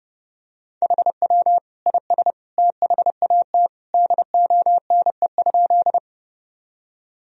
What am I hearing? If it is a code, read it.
Morse code, heard as a keyed tone, about 30 wpm, 700 Hz. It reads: HW IS THAT DONE?
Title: Sentences from Top 500 Words and Abbreviations 30wpm